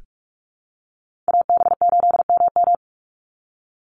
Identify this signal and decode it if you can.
Morse code, heard as a keyed tone, about 44 wpm, 700 Hz. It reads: AB8MM
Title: USA Call Signs 45wpm